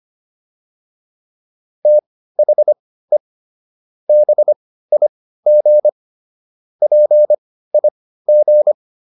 Morse code, heard as a keyed tone, about 25 wpm, spaced as for 15 wpm, 600 Hz. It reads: THE BIG PIG